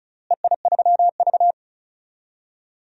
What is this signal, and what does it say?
Morse code, heard as a keyed tone, about 36 wpm, 700 Hz. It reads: EI3V
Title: DXCC-100 Call Signs - 4 Characters - Repeated 3x - 35wpm